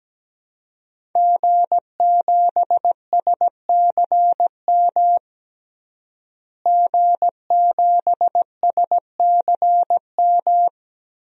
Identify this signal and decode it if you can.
Morse code, heard as a keyed tone, about 17 wpm, 700 Hz. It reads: G7SCM G7SCM
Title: DXCC-100 Call Signs - 5 Characters - Repeated 2x - 17wpm